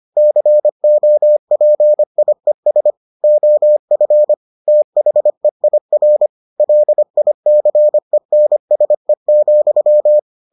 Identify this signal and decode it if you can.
Morse code, heard as a keyed tone, about 25 wpm, 600 Hz. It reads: COPIES OF THEIR LICENSE,